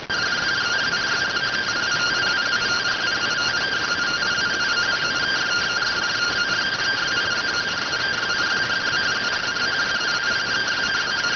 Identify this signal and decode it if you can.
Two-tone teleprinter signal, about 170 Hz shift.